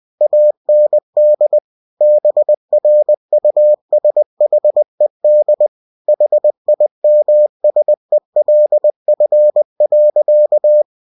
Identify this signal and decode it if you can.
Morse code, heard as a keyed tone, about 20 wpm, 600 Hz. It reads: AND BRUSHED HIMSELF.
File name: cw-books / winnie-the-pooh / winnie-20wpm-page-0076